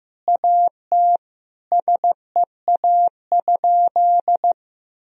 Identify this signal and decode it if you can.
Morse code, heard as a keyed tone, about 16 wpm, 700 Hz. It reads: AT SEA?
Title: Sentences from Top 1000 Words - Review Part 2 - 15wpm